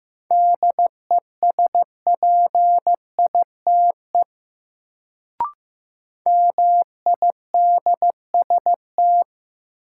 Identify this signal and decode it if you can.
Morse code, heard as a keyed tone, about 15 wpm, 700 Hz. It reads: DESPITE MIDST
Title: Sets of 4 Words - New Word Drill - 15wpm